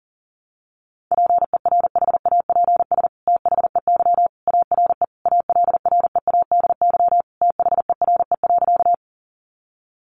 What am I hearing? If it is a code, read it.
Morse code, heard as a keyed tone, about 40 wpm, 700 Hz. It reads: PERHAPS THEY ARE ALREADY THERE.
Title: Sentences from Top 700 Words 40wpm